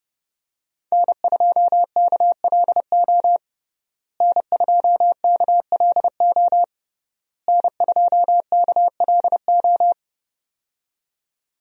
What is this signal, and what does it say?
Morse code, heard as a keyed tone, about 30 wpm, 700 Hz. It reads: N2KLO N2KLO N2KLO
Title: USA Call Signs - 5 Characters - Repeated 3x - 30wpm